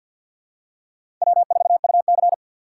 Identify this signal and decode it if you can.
Morse code, heard as a keyed tone, about 53 wpm, 700 Hz. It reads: W4UC
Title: USA Call Signs - 4 Characters - 2x Character Spacing - 50wpm